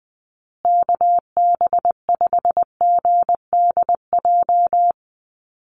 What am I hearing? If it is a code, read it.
Morse code, heard as a keyed tone, about 20 wpm, 700 Hz. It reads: KB5GDJ
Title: USA Call Signs Repeated 2x 20wpm